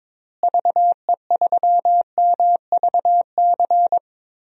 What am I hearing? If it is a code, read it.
Morse code, heard as a keyed tone, about 22 wpm, 700 Hz. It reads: VE3MVC
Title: DXCC-100 Call Signs - 6 Characters - Repeated 3x - 22wpm